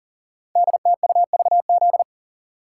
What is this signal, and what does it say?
Morse code, heard as a keyed tone, about 38 wpm, 690 Hz. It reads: DTUVZ
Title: Code Groups - 5 Letters - 40wpm